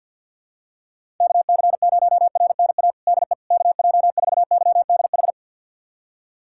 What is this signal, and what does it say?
Morse code, heard as a keyed tone, about 48 wpm, 700 Hz. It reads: KC0RNA DE KJ4YDH
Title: USA Callsign DE Callsign 50wpm